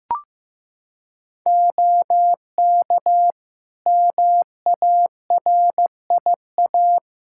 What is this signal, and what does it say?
Morse code, heard as a keyed tone, about 15 wpm, 700 Hz. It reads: OK MARIA
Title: Practice QSO - Exchange 3 - More Info - 15wpm